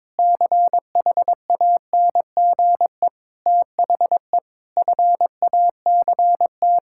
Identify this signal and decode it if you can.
Morse code, heard as a keyed tone, about 22 wpm, 700 Hz. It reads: CHANGE THE FACT
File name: Sentences with 7-Word Phrases - 22wpm